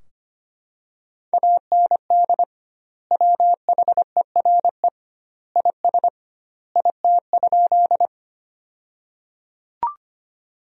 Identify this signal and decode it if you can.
Morse code, heard as a keyed tone, about 25 wpm, 700 Hz. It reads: AND WHERE IS IT?
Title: 2x Word-Spacing - Sentences from Top 200 Words - 25wpm